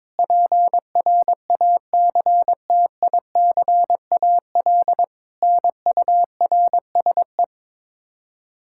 Morse code, heard as a keyed tone, about 22 wpm, 700 Hz. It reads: PRACTICAL NURSE
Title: Practice QSO - Exchange 3 - More Info - 22wpm